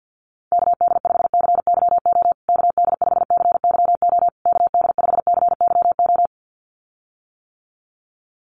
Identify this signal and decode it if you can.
Morse code, heard as a keyed tone, about 50 wpm, 700 Hz. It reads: KD5CYO KD5CYO KD5CYO
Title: USA Call Signs Repeated 3x 50wpm